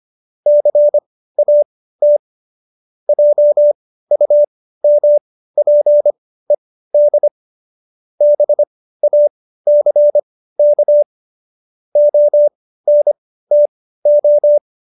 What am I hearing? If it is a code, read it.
Morse code, heard as a keyed tone, about 25 wpm, spaced as for 15 wpm, 590 Hz. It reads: CAT JUMPED BACK ONTO